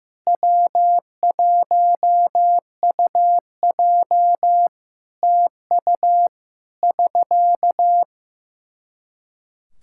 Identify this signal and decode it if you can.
Morse code, heard as a keyed tone, about 15 wpm, 700 Hz. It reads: W1UJ TU <SK>
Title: K1USN SST QSO Practice 15wpm